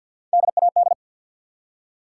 Morse code, heard as a keyed tone, about 53 wpm, 700 Hz. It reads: DAD
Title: Three Letter Words - 50wpm